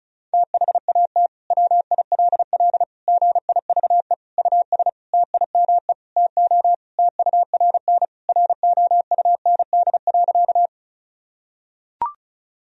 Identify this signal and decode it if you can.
Morse code, heard as a keyed tone, about 34 wpm, 700 Hz. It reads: THAT WILL GIVE US TIME TO TURN ROUND.
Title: Sentences from Top 1000 Words - Review Part 1 - 35wpm